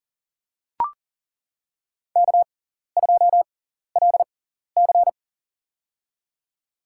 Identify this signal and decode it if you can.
Morse code, heard as a keyed tone, about 40 wpm, spaced as for 14 wpm, 700 Hz. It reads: K2LC
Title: USA Call Signs - 4 Characters - 6x Character Spacing - 40wpm